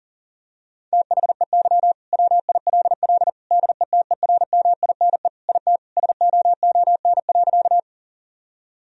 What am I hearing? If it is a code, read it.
Morse code, heard as a keyed tone, about 37 wpm, 700 Hz. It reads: THEY WILL DETERMINE IT SOON.
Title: Sentences from Top 1000 Words - Review Part 4 - 40wpm